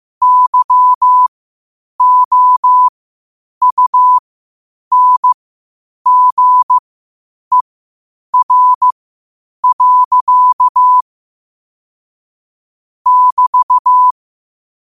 Morse code, heard as a keyed tone, about 15 wpm, spaced as for 8 wpm, 1000 Hz. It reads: YOUNGER. <BT>
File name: qotdfiles / 8wpm